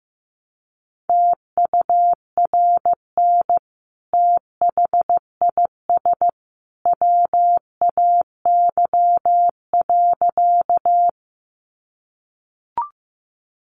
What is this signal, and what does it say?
Morse code, heard as a keyed tone, about 14 wpm, 700 Hz. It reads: TURN THIS WAY.
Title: Sentences from Top 200 Words 15wpm